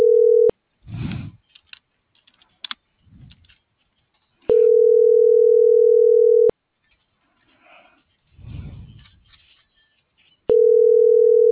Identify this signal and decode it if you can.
Ringback tone (North America), 440 + 480 Hz.